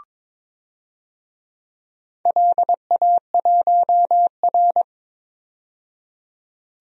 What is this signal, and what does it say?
Morse code, heard as a keyed tone, about 21 wpm, 700 Hz. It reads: LA1R
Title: DXCC-100 Call Signs - 4 Characters - 22wpm